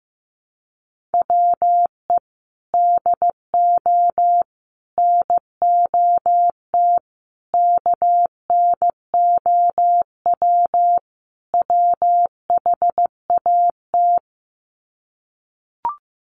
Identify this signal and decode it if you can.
Morse code, heard as a keyed tone, about 15 wpm, 700 Hz. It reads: WE DO NOT KNOW WHAT
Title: Sets of 5 Words 15wpm